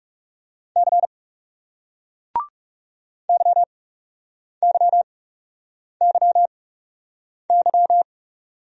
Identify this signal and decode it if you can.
Morse code, heard as a keyed tone, about 38 wpm, spaced as for 8 wpm, 700 Hz. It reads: C YYYY